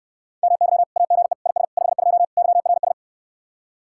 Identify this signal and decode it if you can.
Morse code, heard as a keyed tone, about 69 wpm, 700 Hz. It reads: MY AGE IS 52 YRS